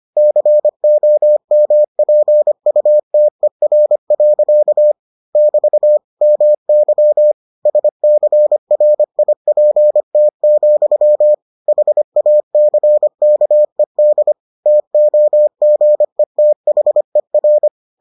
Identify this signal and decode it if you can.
Morse code, heard as a keyed tone, about 25 wpm, 600 Hz. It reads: COMPUTER. <BT> MY SCRIPT, HACKED TOGETHER